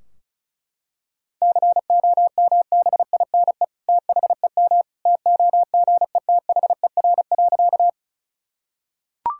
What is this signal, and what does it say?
Morse code, heard as a keyed tone, about 34 wpm, 700 Hz. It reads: COMBINE THEM TOGETHER.
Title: Sentences from Top 1600 Words 35wpm